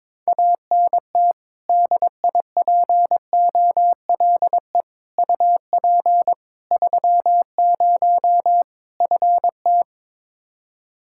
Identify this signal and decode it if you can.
Morse code, heard as a keyed tone, about 22 wpm, 700 Hz. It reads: ANT DIPOLE UP 30 FT
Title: K1USN SST QSO Practice 22wpm